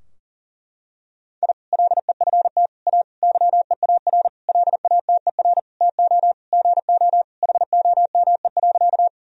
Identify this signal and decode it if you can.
Morse code, heard as a keyed tone, about 40 wpm, 700 Hz. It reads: I LEFT A YEAR LATER TO GO HOME.
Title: Sentences from Top 1000 Words - Review Part 4 - 40wpm